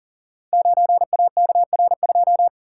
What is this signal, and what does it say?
Morse code, heard as a keyed tone, about 41 wpm, 690 Hz. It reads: 9AKR2
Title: Code Groups - 5 Letter:Numbers - 40wpm